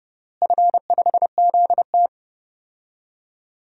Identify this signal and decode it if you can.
Morse code, heard as a keyed tone, about 30 wpm, 700 Hz. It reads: F5ZT
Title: DXCC-100 Call Signs - 4 Characters - Repeated 3x - 30wpm